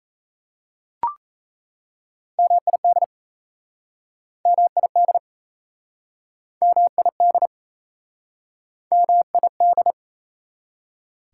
Morse code, heard as a keyed tone, about 38 wpm, 690 Hz. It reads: MID MID MID MID